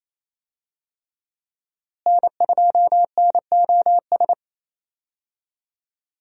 Morse code, heard as a keyed tone, about 28 wpm, 700 Hz. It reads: N2NOS
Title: USA Call Signs - 5 Characters - Repeated 3x - 28wpm